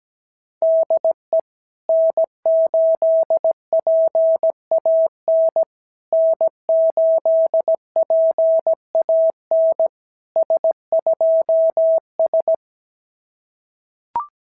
Morse code, heard as a keyed tone, about 17 wpm, 650 Hz. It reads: DE N8PAN N8PAN S2S
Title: SOTA QSO Practice 17wpm